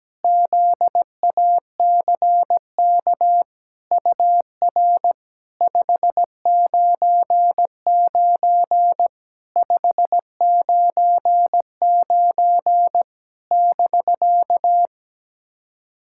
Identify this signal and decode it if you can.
Morse code, heard as a keyed tone, about 17 wpm, 700 Hz. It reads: ZACK UR 599 599 <BK>
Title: SOTA QSO Practice 17wpm